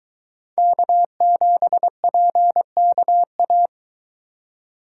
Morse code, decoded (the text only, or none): K7PKA